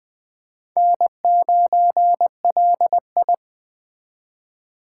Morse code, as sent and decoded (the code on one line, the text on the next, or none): -. ----. .-.. ..
N9LI